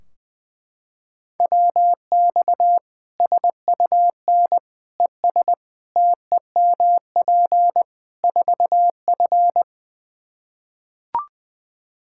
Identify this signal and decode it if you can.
Morse code, heard as a keyed tone, about 20 wpm, 700 Hz. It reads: WX SUN ES TEMP 4F
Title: Practice QSO - Exchange 2 - Info - 20wpm